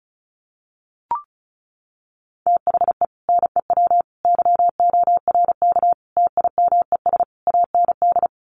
Morse code, heard as a keyed tone, about 34 wpm, 700 Hz. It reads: THE NEW YORK TIMES AND
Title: Sets of 5 Words Encore 35wpm